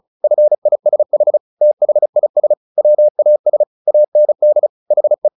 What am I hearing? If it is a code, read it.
Morse code, heard as a keyed tone, about 37 wpm, 600 Hz. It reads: FISH THIS WAS AND HE
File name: the old man and the sea / the old man and the sea 1 35wpm